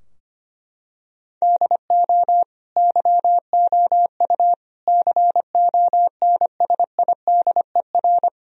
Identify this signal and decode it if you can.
Morse code, heard as a keyed tone, about 25 wpm, 700 Hz. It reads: DO YOU CONSIDER
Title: Sentences from Top 1000 Words - Review Part 4 - 25wpm